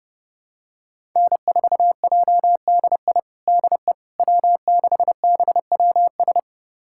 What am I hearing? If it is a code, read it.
Morse code, heard as a keyed tone, about 30 wpm, 700 Hz. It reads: N4JDI DE W6BWS
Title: USA Callsign DE Callsign - 5 Characters - 30wpm